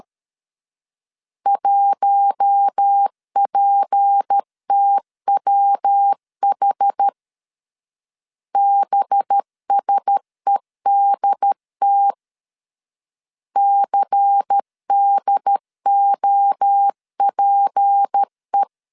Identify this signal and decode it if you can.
Morse code, heard as a keyed tone, about 13 wpm, 800 Hz. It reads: 1PTWH BSEDT CDOPE
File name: LEZIONI / 6 PKX0 / 6 P 12wpm 2033